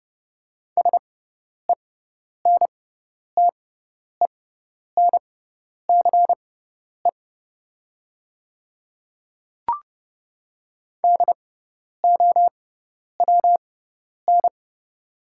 Morse code, heard as a keyed tone, about 30 wpm, spaced as for 10 wpm, 700 Hz. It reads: SENTENCE DOWN
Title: 6x Character Spacing - Top 300 Words - 30wpm